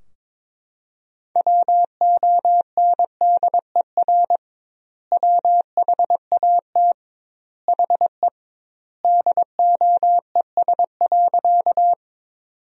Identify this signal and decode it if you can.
Morse code, heard as a keyed tone, about 21 wpm, 700 Hz. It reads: WONDER WHAT HE DOES.
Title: 2x Word-Spacing - Sentences from Top 500 Words - 22wpm